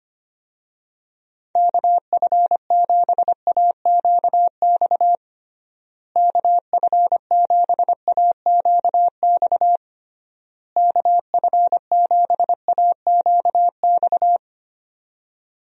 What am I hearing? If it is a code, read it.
Morse code, heard as a keyed tone, about 25 wpm, 700 Hz. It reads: KF7AQX KF7AQX KF7AQX
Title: USA Call Signs - 6 Characters - Repeated 3x - 25wpm